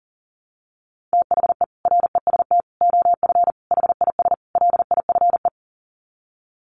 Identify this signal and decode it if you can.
Morse code, heard as a keyed tone, about 40 wpm, 700 Hz. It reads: THE REST OF HIS LIFE
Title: Sets of 5 Words 40wpm